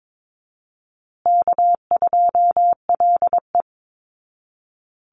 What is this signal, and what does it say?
Morse code, heard as a keyed tone, about 22 wpm, 700 Hz. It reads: K2LE
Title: Easy Call Signs Repeated 3x - 22wpm